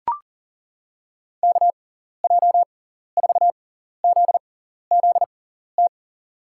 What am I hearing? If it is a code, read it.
Morse code, heard as a keyed tone, about 40 wpm, spaced as for 14 wpm, 700 Hz. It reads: KJ4ZZT
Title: USA Call Signs - 6 Characters - 6x Character Spacing - 40wpm